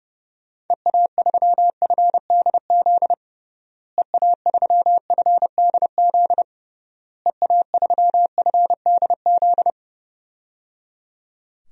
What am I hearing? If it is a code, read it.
Morse code, heard as a keyed tone, about 30 wpm, 700 Hz. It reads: EA3FDZ EA3FDZ EA3FDZ
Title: DXCC-100 Call Signs - 6 Characters - Repeated 3x - 30wpm